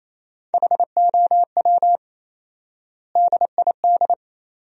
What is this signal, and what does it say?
Morse code, heard as a keyed tone, about 28 wpm, 700 Hz. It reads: HOW DID
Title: 4x Word-Spacing - Sentences from Top 100 Words - 28wpm